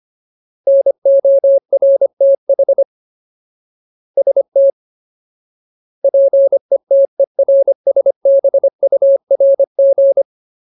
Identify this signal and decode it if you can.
Morse code, heard as a keyed tone, about 25 wpm, 550 Hz. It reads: NORTH ST PETERSBURG
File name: Ultimate Word List / 25WPM Ultimate